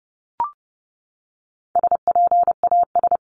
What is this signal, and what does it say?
Morse code, heard as a keyed tone, about 30 wpm, 700 Hz. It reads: SPAS
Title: CWT Exchange 30wpm